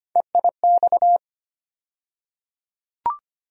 Morse code, heard as a keyed tone, about 25 wpm, 700 Hz. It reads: EIX
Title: USA Call Sign Postfixes - 3 Letters 25wpm